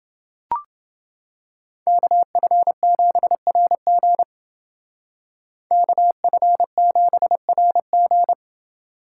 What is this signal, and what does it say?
Morse code, heard as a keyed tone, about 28 wpm, 700 Hz. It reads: KF7RG KF7RG